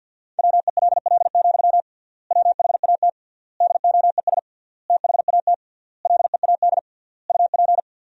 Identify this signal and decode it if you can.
Morse code, heard as a keyed tone, about 52 wpm, 700 Hz. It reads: WELL, WHAT DOES THAT LEAD UP
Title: 3x Word-Spacing - Sentences from Top 500 Words - 50wpm